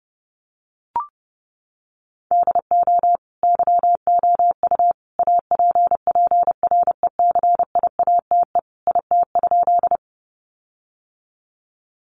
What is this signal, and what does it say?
Morse code, heard as a keyed tone, about 30 wpm, 700 Hz. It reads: DO YOU APPRECIATE IT?
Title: Sentences from Top 1300 Words 30wpm